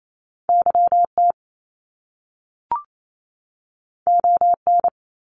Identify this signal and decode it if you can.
Morse code, heard as a keyed tone, about 28 wpm, 700 Hz. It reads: YT ON
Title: Top Two Letter Combinations in Words 28wpm